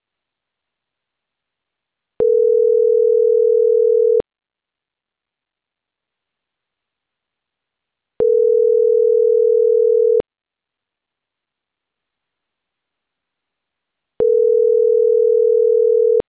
Ringback tone (North America), 440 + 480 Hz.
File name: CENTRAL CAEA / out agcaea01 - .56229